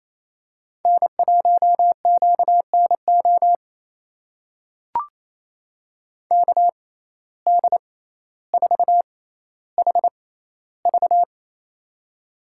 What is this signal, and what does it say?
Morse code, heard as a keyed tone, about 28 wpm, spaced as for 14 wpm, 700 Hz. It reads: N1QNO KD4HV
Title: USA Call Signs - 5 Characters - 6x Character Spacing - 28wpm